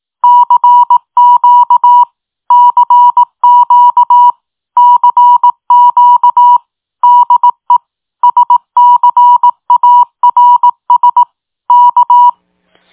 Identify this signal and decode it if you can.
Morse code, heard as a keyed tone, about 18 wpm, 970 Hz. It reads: CQ CQ CQ DE SCARS K